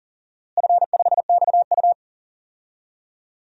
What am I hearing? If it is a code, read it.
Morse code, heard as a keyed tone, about 40 wpm, 700 Hz. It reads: F5XU
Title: DXCC-100 Call Signs - 4 Characters - Repeated 2x - 40wpm